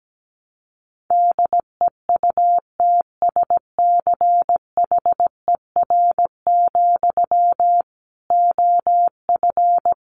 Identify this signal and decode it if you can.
Morse code, heard as a keyed tone, about 17 wpm, 700 Hz. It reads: DEUTSCHER, OF
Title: A Sound Of Thunder 17wpm